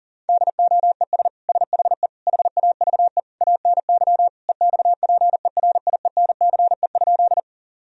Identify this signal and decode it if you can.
Morse code, heard as a keyed tone, about 40 wpm, 700 Hz. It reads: DOES SHE HAVE ANY EXPERIENCE?
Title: Sentences from Top 1000 Words - Review Part 3 - 40wpm